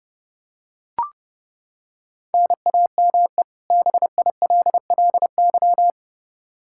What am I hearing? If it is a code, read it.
Morse code, heard as a keyed tone, about 30 wpm, 700 Hz. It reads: NAME BILLY
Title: Practice QSO - Exchange 1 - Beginning - 30wpm